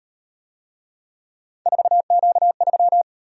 Morse code, heard as a keyed tone, about 36 wpm, 700 Hz. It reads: 4Q3